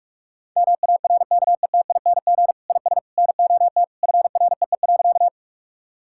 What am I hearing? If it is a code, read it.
Morse code, heard as a keyed tone, about 46 wpm, 700 Hz. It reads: MARKETING IS NOT FREE.
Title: Sentences from Top 1000 Words - Review Part 5 - 45wpm